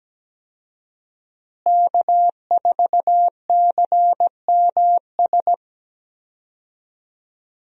Morse code, heard as a keyed tone, about 17 wpm, 700 Hz. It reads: K4CMS
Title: USA Call Signs - 5 Characters - Repeated 2x - 17wpm